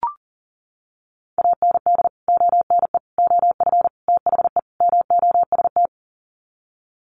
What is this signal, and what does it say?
Morse code, heard as a keyed tone, about 40 wpm, 700 Hz. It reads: AND ONE OF THE MOST
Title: Sets of 5 Words Encore 40wpm